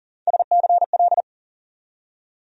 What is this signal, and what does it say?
Morse code, heard as a keyed tone, about 40 wpm, 700 Hz. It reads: SCL
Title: USA Call Sign Postfixes - 3 Letters 40wpm